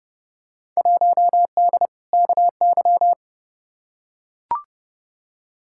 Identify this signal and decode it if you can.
Morse code, heard as a keyed tone, about 30 wpm, 700 Hz. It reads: 1D KY
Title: Field Day Exchanges - 30wpm